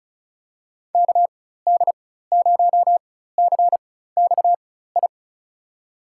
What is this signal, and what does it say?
Morse code, heard as a keyed tone, about 36 wpm, spaced as for 16 wpm, 700 Hz. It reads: KD0CXI
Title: USA Call Signs - 4x Character Spacing - 35wpm